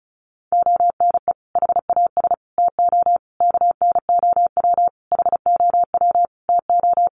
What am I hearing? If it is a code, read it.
Morse code, heard as a keyed tone, about 34 wpm, 700 Hz. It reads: ONE HAS TO KNOW HOW TO
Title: Sentences from Top 800 Words 35wpm